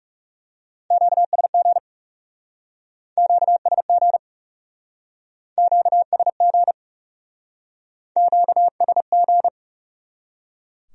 Morse code, heard as a keyed tone, about 39 wpm, 700 Hz. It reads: QSG QSG QSG QSG